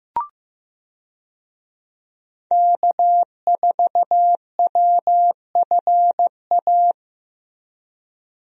Morse code, heard as a keyed tone, about 16 wpm, 700 Hz. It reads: K4WFA
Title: USA Call Signs - 5 Characters - Repeated 2x - 15wpm